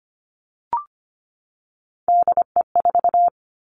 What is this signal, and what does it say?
Morse code, heard as a keyed tone, about 25 wpm, 700 Hz. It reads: DE4
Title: Three Challenging Letter-Number Combinations 25wpm